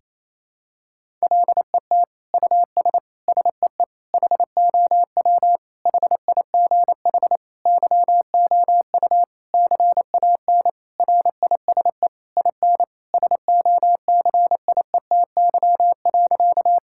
Morse code, heard as a keyed tone, about 28 wpm, 700 Hz. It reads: LET US SEE HOW HIGH YOU CAN RISE IN SOCIETY.
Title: Sentences from Top 1000 Words - Review Part 6 - 28wpm